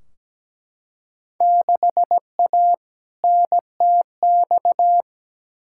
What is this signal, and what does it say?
Morse code, heard as a keyed tone, about 17 wpm, 700 Hz. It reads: 6A NTX
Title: Field Day Exchanges - 17wpm